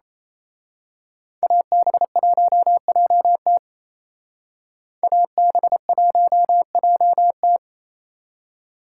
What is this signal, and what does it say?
Morse code, heard as a keyed tone, about 32 wpm, 700 Hz. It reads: AB1JT AB1JT
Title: Speed Racing - USA 2x2 Call Signs 28wpm